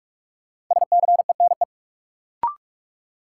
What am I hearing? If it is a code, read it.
Morse code, heard as a keyed tone, about 46 wpm, 700 Hz. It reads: SCENE
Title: Sentences from Top 1100 Words - New Word Drill Alt - 45wpm